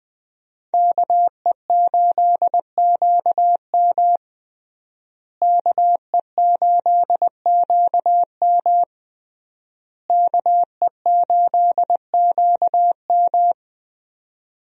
Morse code, heard as a keyed tone, about 20 wpm, 700 Hz. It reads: KE8QM KE8QM KE8QM
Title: USA Call Signs - 5 Characters - Repeated 3x - 20wpm